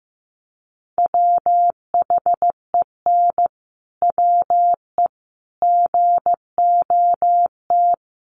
Morse code, heard as a keyed tone, about 15 wpm, 700 Hz. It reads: WHEN WE GOT
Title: Sentences from Top 1400 Words 15wpm